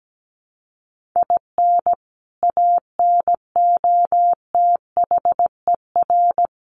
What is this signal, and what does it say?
Morse code, heard as a keyed tone, about 17 wpm, 700 Hz. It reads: IN ANOTHER
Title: Sentences from Top 1500 Words 17wpm